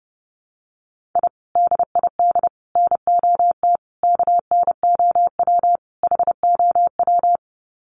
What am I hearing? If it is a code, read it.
Morse code, heard as a keyed tone, about 30 wpm, 700 Hz. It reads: I DID NOT KNOW HOW
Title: Sets of 5 Words Encore 30wpm